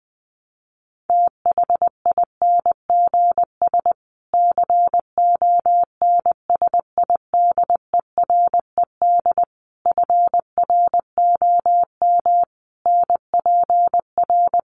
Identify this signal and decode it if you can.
Morse code, heard as a keyed tone, about 20 wpm, 700 Hz. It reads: THINGS CONSIDERED FROM NPR
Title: Sets of 5 Words Encore 20wpm